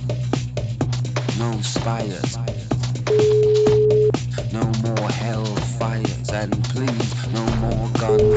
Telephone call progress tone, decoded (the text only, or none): ringback tone